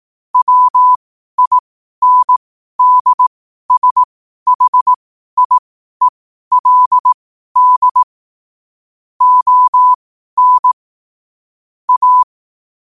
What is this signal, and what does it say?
Morse code, heard as a keyed tone, about 18 wpm, spaced as for 12 wpm, 1000 Hz. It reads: WINDSHIELD ON A
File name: qotdfiles / 12wpm